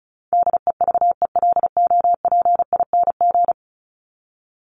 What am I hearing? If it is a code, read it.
Morse code, heard as a keyed tone, about 34 wpm, 700 Hz. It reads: DEVELOPING
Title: ING Suffix - Top 100 Words with Base Words 35wpm